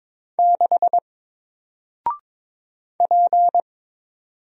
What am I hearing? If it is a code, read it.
Morse code, heard as a keyed tone, about 22 wpm, 700 Hz. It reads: 6 P
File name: Single-Letter-Number-22wpm